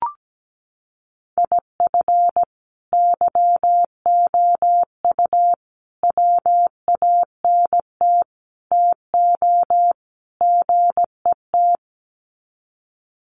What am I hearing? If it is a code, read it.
Morse code, heard as a keyed tone, about 17 wpm, 700 Hz. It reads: IF YOU WANT TO GET
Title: Sets of 5 Words Encore 17wpm